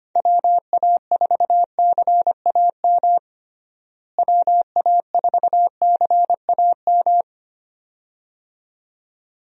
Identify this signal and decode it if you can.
Morse code, heard as a keyed tone, about 25 wpm, 700 Hz. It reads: WA4CAM WA4CAM
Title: USA Call Signs - 6 Characters - Repeated 2x - 25wpm